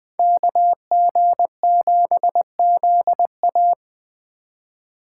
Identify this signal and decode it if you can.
Morse code, heard as a keyed tone, about 21 wpm, 700 Hz. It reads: KG7ZA
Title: Speed Racing - USA Call Signs 17wpm